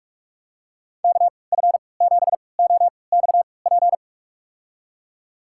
Morse code, heard as a keyed tone, about 44 wpm, spaced as for 25 wpm, 700 Hz. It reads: KF7OXP
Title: USA Call Signs - 3x Character Spacing - 45wpm